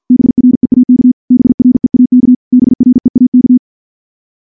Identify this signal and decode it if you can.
Morse code, heard as a keyed tone, about 49 wpm, 250 Hz. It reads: BREAK BREAK BREAK
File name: morse / morse-training-50WPM